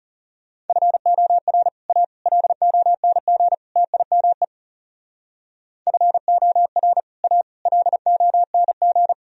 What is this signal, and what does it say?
Morse code, heard as a keyed tone, about 37 wpm, 700 Hz. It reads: FOR A LONG TIME FOR A LONG
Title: Speed-Racing - Sets of 4 Words 30wpm